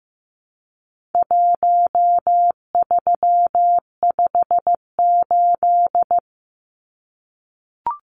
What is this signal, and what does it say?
Morse code, heard as a keyed tone, about 15 wpm, 700 Hz. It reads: 1358
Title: Numbers - Contest Style - 4-Digit 15wpm